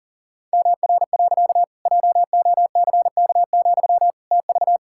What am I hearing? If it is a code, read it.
Morse code, heard as a keyed tone, about 40 wpm, 700 Hz. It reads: MR. JOCK, TV
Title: Pangram Challenge v2 - 40wpm